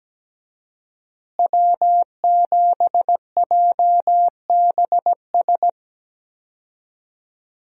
Morse code, heard as a keyed tone, about 17 wpm, 700 Hz. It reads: W7JBS
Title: USA Call Signs - 5 Characters - Repeated 2x - 17wpm